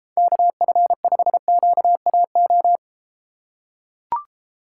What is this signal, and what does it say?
Morse code, heard as a keyed tone, about 33 wpm, 700 Hz. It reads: KF5QAO